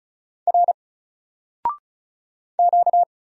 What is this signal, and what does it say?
Morse code, heard as a keyed tone, about 36 wpm, 700 Hz. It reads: R Q
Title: Single Character (Letter:Number:Punctuation:Prosign) 35wpm